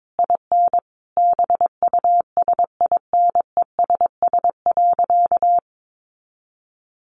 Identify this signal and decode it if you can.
Morse code, heard as a keyed tone, about 22 wpm, 700 Hz. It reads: IN BUSINESS.
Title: Sentences from Top 900 Words 22wpm